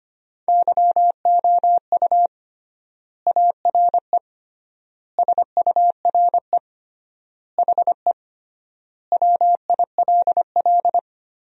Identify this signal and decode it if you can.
Morse code, heard as a keyed tone, about 25 wpm, 700 Hz. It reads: YOU ARE SURE HE WILL
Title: 3x Word-Spacing - Sentences from Top 300 Words - 25wpm